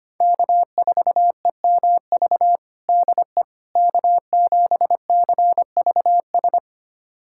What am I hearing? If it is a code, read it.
Morse code, heard as a keyed tone, about 25 wpm, 700 Hz. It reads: K4EMV DE K7CVS
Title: USA Callsign DE Callsign - 5 Characters - 25wpm